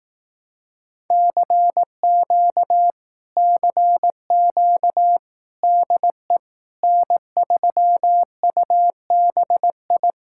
Morse code, heard as a keyed tone, about 18 wpm, 700 Hz. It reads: CQ CQ DE N3UBI
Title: Speed-Racing Common QSO Phrases Encore - 15wpm